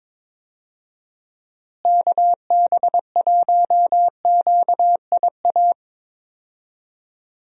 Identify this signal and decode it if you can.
Morse code, heard as a keyed tone, about 22 wpm, 700 Hz. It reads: KB1QIA
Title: USA Call Signs - 6 Characters - Repeated 3x - 22wpm